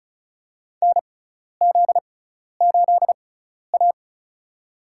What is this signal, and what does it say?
Morse code, heard as a keyed tone, about 34 wpm, spaced as for 12 wpm, 700 Hz. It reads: NZ8A
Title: USA Call Signs - 4 Characters - 6x Character Spacing - 35wpm